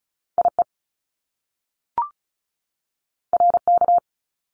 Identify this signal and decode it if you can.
Morse code, heard as a keyed tone, about 36 wpm, 700 Hz. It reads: IE RK
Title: Top Two Letter Combinations in Words 35wpm